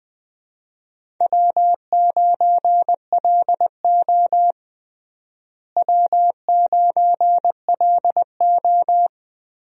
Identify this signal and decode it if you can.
Morse code, heard as a keyed tone, about 20 wpm, 700 Hz. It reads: W9LO W9LO
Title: USA Call Signs - 4 Characters - Repeated 2x - 20wpm